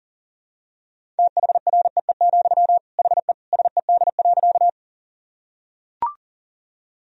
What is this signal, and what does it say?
Morse code, heard as a keyed tone, about 40 wpm, 700 Hz. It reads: THREE, HE SED.
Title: Sentences from Top 200 Words and Abbreviations 40wpm